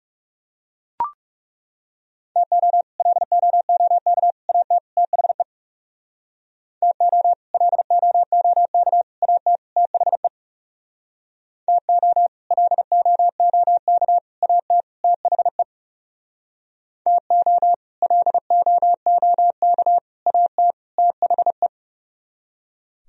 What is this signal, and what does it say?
Morse code, heard as a keyed tone, about 37 wpm, 700 Hz. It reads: TO LOOK AT THE TO LOOK AT THE TO LOOK AT THE TO LOOK AT THE
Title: Speed-Racing - Sets of 4 Words 30wpm